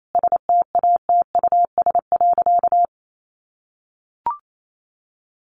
Morse code, heard as a keyed tone, about 28 wpm, 700 Hz. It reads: STATUS.
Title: Sentences from Top 1400 Words 28wpm